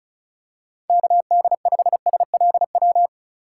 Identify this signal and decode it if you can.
Morse code, heard as a keyed tone, about 36 wpm, 700 Hz. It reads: KD5SLW
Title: USA Call Signs - 6 Characters - 2x Character Spacing - 35wpm